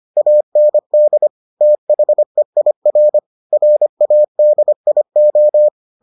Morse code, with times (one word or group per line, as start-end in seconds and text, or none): 0.2-1.3 AND
1.6-3.2 THEIR
3.5-5.7 RADIO